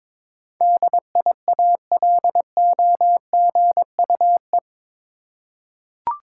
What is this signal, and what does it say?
Morse code, heard as a keyed tone, about 22 wpm, 700 Hz. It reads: DIALOGUE
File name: practice / 22wpm / Eight Letter Words - 22wpm